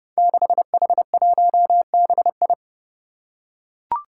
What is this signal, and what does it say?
Morse code, heard as a keyed tone, about 28 wpm, 700 Hz. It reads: 6H1BI
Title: Five Challenging Letter-Number Combinations 30wpm